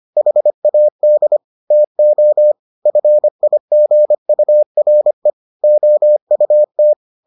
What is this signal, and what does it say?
Morse code, heard as a keyed tone, about 25 wpm, 600 Hz. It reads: HAD TO FIGURE OUT